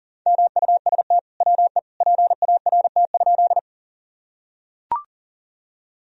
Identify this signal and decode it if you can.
Morse code, heard as a keyed tone, about 38 wpm, 700 Hz. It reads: MUST WE PART?
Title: Sentences from Top 1000 Words - Review Part 1 - 40wpm